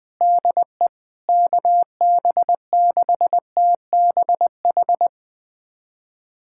Morse code, heard as a keyed tone, about 21 wpm, 700 Hz. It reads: DE KB6TBH
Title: USA Callsign DE Callsign - 6 Characters - 20wpm